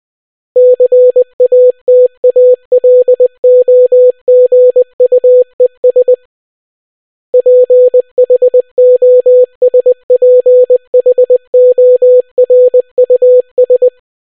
Morse code, heard as a keyed tone, about 20 wpm, 500 Hz. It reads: CATALOGUES PHOSPHORUS